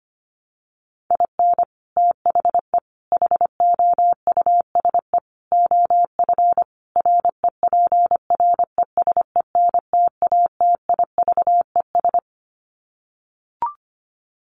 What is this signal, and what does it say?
Morse code, heard as a keyed tone, about 25 wpm, 700 Hz. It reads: IN THE HOUSE OF REPRESENTATIVES
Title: Sets of 5 Words Encore 25wpm